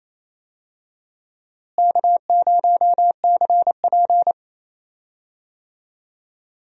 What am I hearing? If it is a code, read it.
Morse code, heard as a keyed tone, about 28 wpm, 700 Hz. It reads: K0CP
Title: USA Call Signs - 4 Characters - Repeated 3x - 28wpm